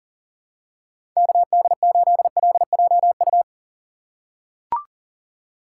Morse code, heard as a keyed tone, about 37 wpm, 700 Hz. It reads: KD8LJU
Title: USA Call Signs - 6 Characters - 2x Character Spacing - 40wpm